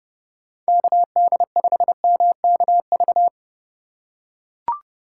Morse code, heard as a keyed tone, about 30 wpm, 700 Hz. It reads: KD5MKV
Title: USA Call Signs - 6 Characters - 2x Character Spacing - 30wpm